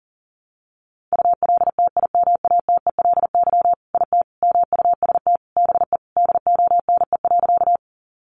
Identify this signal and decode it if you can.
Morse code, heard as a keyed tone, about 40 wpm, 700 Hz. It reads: ULTIMATELY IT MUST BE DONE.
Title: Sentences from Top 1200 Words 40wpm